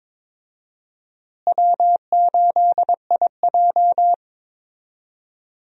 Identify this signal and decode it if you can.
Morse code, heard as a keyed tone, about 22 wpm, 700 Hz. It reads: W8IJ
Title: USA Call Signs - 4 Characters - Repeated 3x - 22wpm